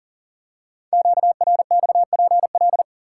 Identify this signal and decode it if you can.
Morse code, heard as a keyed tone, about 40 wpm, 700 Hz. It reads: QRXPL
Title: Code Groups - 5 Letters - 40wpm